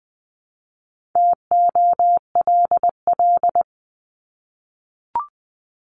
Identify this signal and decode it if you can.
Morse code, heard as a keyed tone, about 20 wpm, 700 Hz. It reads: TOLL
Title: 260 Lesson 42e - Words 20-20wpm